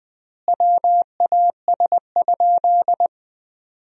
Morse code, heard as a keyed tone, about 20 wpm, 700 Hz. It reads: WAS?
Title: 3x Word-Spacing - Sentences from Top 200 Words - 20wpm